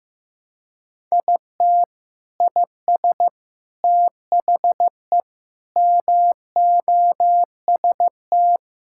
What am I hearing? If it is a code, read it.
Morse code, heard as a keyed tone, about 15 wpm, 700 Hz. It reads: IT IS THE MOST
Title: Sentences from Top 1600 Words 15wpm